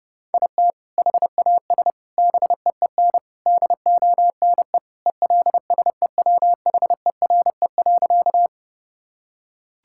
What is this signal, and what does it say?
Morse code, heard as a keyed tone, about 32 wpm, 700 Hz. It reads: IT HAS BEEN DONE ELSEWHERE.
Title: Sentences from Top 1600 Words 30wpm